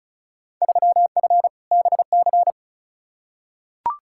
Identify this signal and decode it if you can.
Morse code, heard as a keyed tone, about 34 wpm, 700 Hz. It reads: 3F BC
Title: Field Day Exchanges Encore - 35wpm